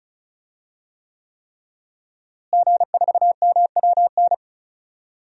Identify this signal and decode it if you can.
Morse code, heard as a keyed tone, about 36 wpm, 700 Hz. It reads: G4MWN
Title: DXCC-100 Call Signs - 5 Characters - Repeated 3x - 35wpm